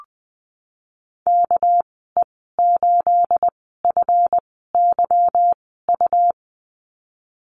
Morse code, heard as a keyed tone, about 20 wpm, spaced as for 15 wpm, 700 Hz. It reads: KE8FYU
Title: 219 Lesson 35g - Number 0 - Call Signs 20wpm